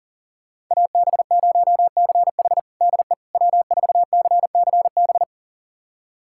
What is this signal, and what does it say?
Morse code, heard as a keyed tone, about 37 wpm, 700 Hz. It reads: AB0CH DE W4CCB
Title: USA Callsign DE Callsign - 5 Characters - 40wpm